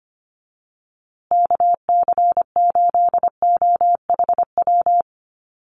Morse code, heard as a keyed tone, about 25 wpm, 700 Hz. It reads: KC8OHW
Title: USA Call Signs Repeated 3x 25wpm